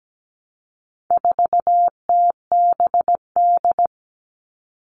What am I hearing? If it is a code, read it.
Morse code, heard as a keyed tone, about 17 wpm, 700 Hz. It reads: 4TBD
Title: All Cut Numbers - 4-Digit 17wpm